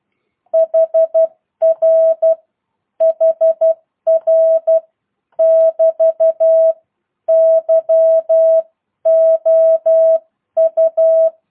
Morse code, heard as a keyed tone, about 12 wpm, 650 Hz. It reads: HR HR <BT> YOU